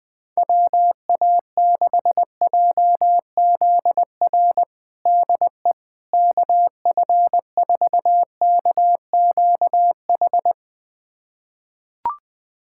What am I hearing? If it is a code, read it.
Morse code, heard as a keyed tone, about 20 wpm, 700 Hz. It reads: WA6JZR DE KF4KQH
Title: USA Callsign DE Callsign - 6 Characters - 20wpm